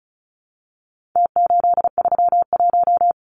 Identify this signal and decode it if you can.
Morse code, heard as a keyed tone, about 34 wpm, 700 Hz. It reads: T831
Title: Numbers - Contest Style - 4-Digit 35wpm